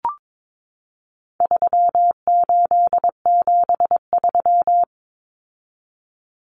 Morse code, heard as a keyed tone, about 22 wpm, 700 Hz. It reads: 3873
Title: Numbers - 4-Digit 22wpm